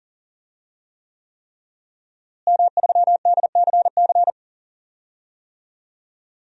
Morse code, heard as a keyed tone, about 40 wpm, 700 Hz. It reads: M3DCC
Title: DXCC-100 Call Signs - 5 Characters - Repeated 3x - 40wpm